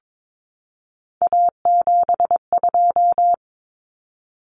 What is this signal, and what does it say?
Morse code, heard as a keyed tone, about 22 wpm, 700 Hz. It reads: A72
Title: Common Cut Numbers - 3-Digit 22wpm